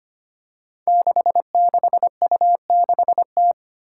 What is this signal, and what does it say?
Morse code, heard as a keyed tone, about 25 wpm, 700 Hz. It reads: 66U6T
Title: Five Challenging Letter-Number Combinations 25wpm